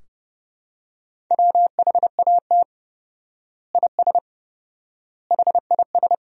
Morse code, heard as a keyed tone, about 32 wpm, 700 Hz. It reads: WHAT IS HIS
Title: 4x Word-Spacing - Sentences from Top 200 Words - 30wpm